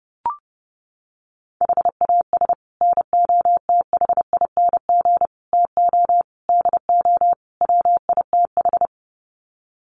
Morse code, heard as a keyed tone, about 30 wpm, 700 Hz. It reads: HAS NOTHING TO DO WITH
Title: Sets of 5 Words 30wpm